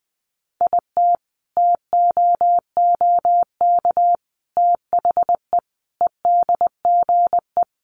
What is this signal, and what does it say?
Morse code, heard as a keyed tone, about 20 wpm, 700 Hz. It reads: IT TOOK THE EDGE
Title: Sentences from Top 1300 Words 20wpm